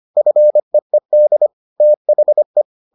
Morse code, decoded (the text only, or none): FEED THE